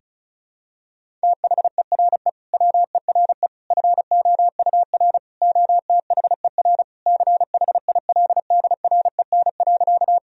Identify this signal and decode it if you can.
Morse code, heard as a keyed tone, about 36 wpm, 690 Hz. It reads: THERE WERE FOUR OTHER CHILDREN.
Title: Sentences from Top 1000 Words - Review Part 1 - 35wpm